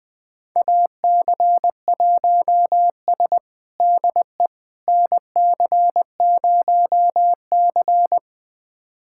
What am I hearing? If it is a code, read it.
Morse code, heard as a keyed tone, about 20 wpm, 700 Hz. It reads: AC1S DE NC0C
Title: USA Callsign DE Callsign - 4 Characters - 20wpm